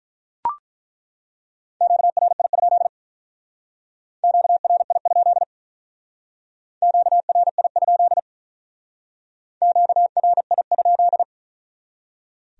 Morse code, heard as a keyed tone, about 43 wpm, 700 Hz. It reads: QRI? QRI? QRI? QRI?
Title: Speed-Racing Q-Codes - 35wpm